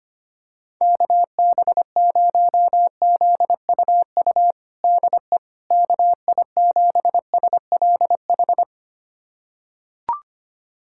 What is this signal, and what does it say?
Morse code, heard as a keyed tone, about 25 wpm, 700 Hz. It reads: KB0ZUU DE KI7SLH
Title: USA Callsign DE Callsign - 6 Characters - 25wpm